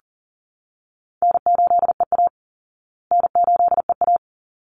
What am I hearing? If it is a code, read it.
Morse code, heard as a keyed tone, about 40 wpm, 700 Hz. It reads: N8EA N8EA
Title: Easy Call Signs Repeated 2x - 40wpm